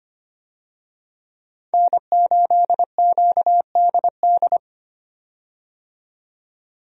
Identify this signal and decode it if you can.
Morse code, heard as a keyed tone, about 25 wpm, 700 Hz. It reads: N8QDD
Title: USA Call Signs - 5 Characters - 25wpm